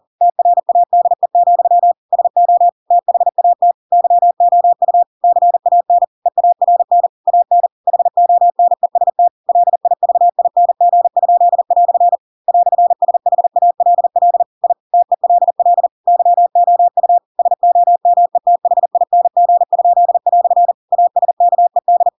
Morse code, heard as a keyed tone, about 40 wpm, 700 Hz. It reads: TRADE, SO THAT YOU CAN EARN AN HONEST LIVING?" "SHALL I TELL YOU SOMETHING?" ASKED
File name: Pinocchio / Pinocchio ch4 40 40 ik1hge